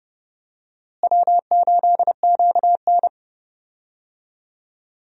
Morse code, heard as a keyed tone, about 30 wpm, 700 Hz. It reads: W8QN
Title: USA Call Signs - 4 Characters - Repeated 3x - 30wpm